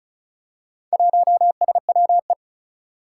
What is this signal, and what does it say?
Morse code, heard as a keyed tone, about 36 wpm, 700 Hz. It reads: 1SWE
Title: Four Challenging Letter-Number Combinations 35wpm